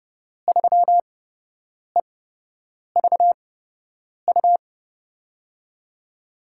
Morse code, heard as a keyed tone, about 30 wpm, spaced as for 8 wpm, 700 Hz. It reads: 3EVU
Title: USA Call Signs - 8x Character Spacing - 30wpm